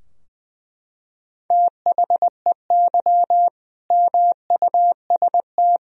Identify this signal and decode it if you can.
Morse code, heard as a keyed tone, about 20 wpm, 700 Hz. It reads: THEY MUST
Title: Sentences from Top 1000 Words - Review Part 3 - 20wpm